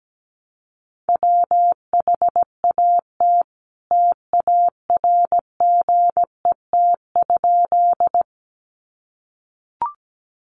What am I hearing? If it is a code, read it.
Morse code, heard as a keyed tone, about 17 wpm, 700 Hz. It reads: WHAT TARGET?
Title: Sentences from Top 1200 Words 17wpm